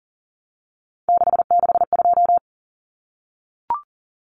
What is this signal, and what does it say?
Morse code, heard as a keyed tone, about 40 wpm, 700 Hz. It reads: 662
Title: Common Cut Numbers - 3-Digit 40wpm